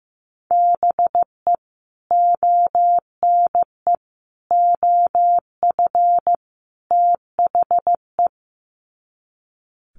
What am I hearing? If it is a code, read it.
Morse code, heard as a keyed tone, about 15 wpm, 700 Hz. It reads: BE ONE OF THE
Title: Sets of 4 Words 15wpm